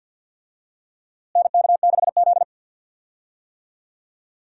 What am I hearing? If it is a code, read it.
Morse code, heard as a keyed tone, about 51 wpm, 700 Hz. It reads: NK6Z
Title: USA Call Signs - 4 Characters - Repeated 3x - 50wpm